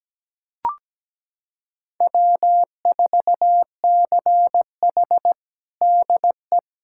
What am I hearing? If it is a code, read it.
Morse code, heard as a keyed tone, about 17 wpm, 700 Hz. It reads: W4CH DE
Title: USA Callsign DE Callsign - 4 Characters - 17wpm